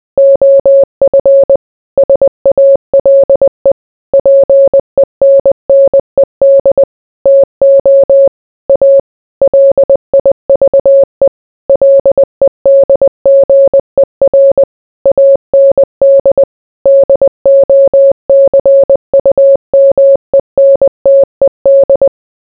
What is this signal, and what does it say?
Morse code, heard as a keyed tone, about 20 wpm, 550 Hz. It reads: OF SALE PENNED TO A LIVE LEDGER AND DOCUMENTED